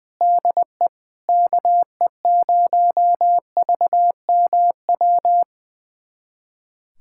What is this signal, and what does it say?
Morse code, heard as a keyed tone, about 21 wpm, 700 Hz. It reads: DE KE0VMW
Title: USA Callsign DE Callsign - 6 Characters - 20wpm